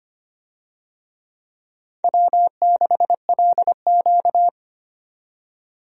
Morse code, heard as a keyed tone, about 25 wpm, 700 Hz. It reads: W6LQ
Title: USA Call Signs - 4 Characters - 25wpm